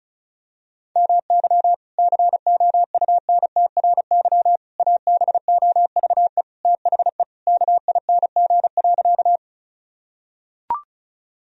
Morse code, heard as a keyed tone, about 36 wpm, 700 Hz. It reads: MY COUNTRY ABOVE THE KING.
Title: Sentences from Top 1000 Words - Review Part 2 - 35wpm